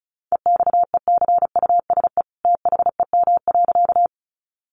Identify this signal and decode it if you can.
Morse code, heard as a keyed tone, about 34 wpm, 700 Hz. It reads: EXECUSE THEM.
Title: Sentences from Top 700 Words 35wpm